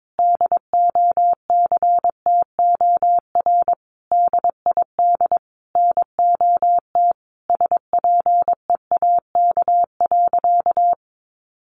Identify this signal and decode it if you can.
Morse code, heard as a keyed tone, about 22 wpm, 700 Hz. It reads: DOCTOR DID NOT SPEAK.
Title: Sentences from Top 1500 Words 22wpm